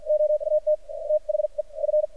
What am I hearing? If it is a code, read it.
Morse code, heard as a keyed tone, about 52 wpm, spaced as for 43 wpm, 610 Hz. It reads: OAT THE F
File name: archived / CRSnet